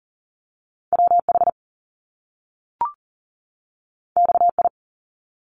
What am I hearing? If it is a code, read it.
Morse code, heard as a keyed tone, about 41 wpm, 700 Hz. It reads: WH XI
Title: Top Two Letter Combinations in Words 40wpm